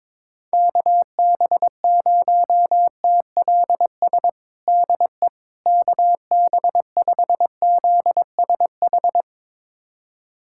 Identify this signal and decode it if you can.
Morse code, heard as a keyed tone, about 22 wpm, 700 Hz. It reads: KB0TLS DE KB5ZSH
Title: USA Callsign DE Callsign - 6 Characters - 22wpm